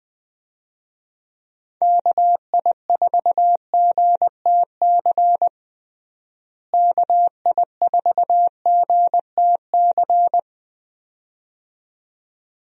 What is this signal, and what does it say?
Morse code, heard as a keyed tone, about 21 wpm, 700 Hz. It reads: KI4GTC KI4GTC
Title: USA Call Signs - 6 Characters - Repeated 2x - 20wpm